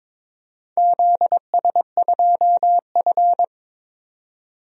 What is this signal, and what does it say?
Morse code, heard as a keyed tone, about 22 wpm, 700 Hz. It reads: ZS2F
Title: DXCC-100 Call Signs - 4 Characters - 22wpm